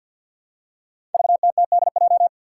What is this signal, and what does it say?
Morse code, heard as a keyed tone, about 53 wpm, 700 Hz. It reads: VTTDJ
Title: Five Challenging Letter-Number Combinations 50wpm